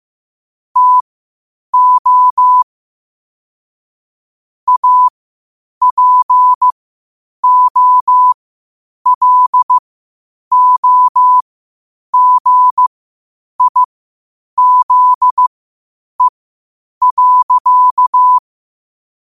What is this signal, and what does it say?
Morse code, heard as a keyed tone, about 15 wpm, spaced as for 8 wpm, 1000 Hz. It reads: TO APOLOGIZE.
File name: qotdfiles / 8wpm